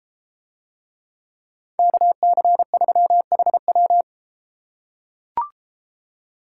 Morse code, heard as a keyed tone, about 30 wpm, 700 Hz. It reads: KC3HW